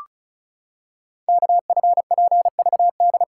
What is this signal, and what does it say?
Morse code, heard as a keyed tone, about 36 wpm, 700 Hz. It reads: KFPVD
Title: Code Groups - 5 Letters - 35wpm